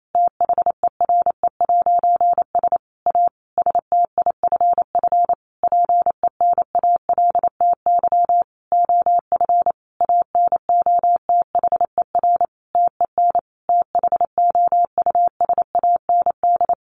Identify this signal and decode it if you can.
Morse code, heard as a keyed tone, about 28 wpm, 700 Hz. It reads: THERE'S A STIFF PENALTY OF ANOTHER TEN THOUSAND
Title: A Sound Of Thunder 28wpm